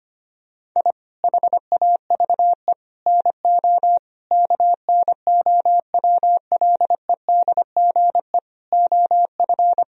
Morse code, heard as a keyed tone, about 25 wpm, 700 Hz. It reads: I HAVE NO KNOWLEDGE OF
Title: Sentences from Top 1000 Words - Review Part 3 - 25wpm